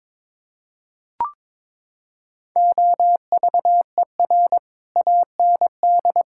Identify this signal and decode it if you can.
Morse code, heard as a keyed tone, about 22 wpm, 700 Hz. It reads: OVER AND
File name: practice / 22wpm / Sets of 6 Words - 22wpm